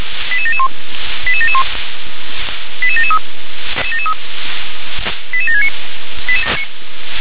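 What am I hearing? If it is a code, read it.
Five-tone selective call (ZVEI), heard as quick runs of five tones, tones of about 80 ms. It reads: 88771 88771 88773 88769
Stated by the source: Received on an SDR, ZVEI five-tone code 88713.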